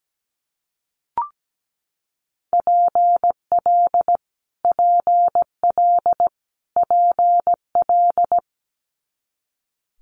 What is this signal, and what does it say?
Morse code, heard as a keyed tone, about 17 wpm, 700 Hz. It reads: PL PL PL
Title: Multi-letter Phonemes Repeated 3x 17wpm